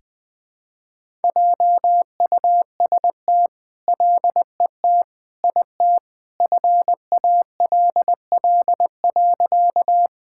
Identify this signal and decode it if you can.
Morse code, heard as a keyed tone, about 21 wpm, 700 Hz. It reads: JUST LET IT FALL.
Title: All-Age Appropriate Jokes - 20wpm